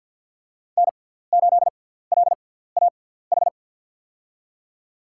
Morse code, heard as a keyed tone, about 50 wpm, spaced as for 17 wpm, 700 Hz. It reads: N8LAH
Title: USA Call Signs - 6x Character Spacing - 50wpm